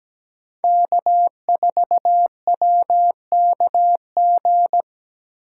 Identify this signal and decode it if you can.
Morse code, heard as a keyed tone, about 17 wpm, 700 Hz. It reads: K4WKG
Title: USA Call Signs - 5 Characters - 3x Character Spacing - 17wpm